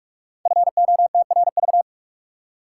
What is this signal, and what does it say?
Morse code, heard as a keyed tone, about 44 wpm, 700 Hz. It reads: FOTRV
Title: Code Groups - 5 Letters - 45wpm